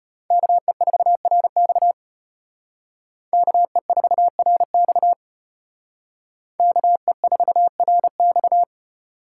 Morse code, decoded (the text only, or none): KE4RX KE4RX KE4RX